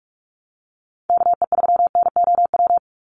Morse code, heard as a keyed tone, about 44 wpm, 700 Hz. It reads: KE3NOW
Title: USA Call Signs 45wpm